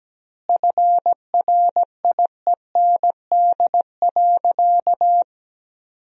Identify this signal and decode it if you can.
Morse code, heard as a keyed tone, about 17 wpm, 700 Hz. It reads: FRIEND.
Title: 2x Word-Spacing - Sentences from Top 300 Words - 17wpm